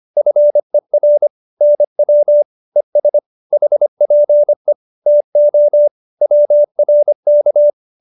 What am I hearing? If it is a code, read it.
Morse code, heard as a keyed tone, about 25 wpm, 590 Hz. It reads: FER NW ES HPE TO WRK